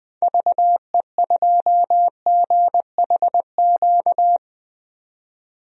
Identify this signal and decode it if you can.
Morse code, heard as a keyed tone, about 20 wpm, 700 Hz. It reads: VE2GHQ
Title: DXCC-100 Call Signs - 6 Characters - 20wpm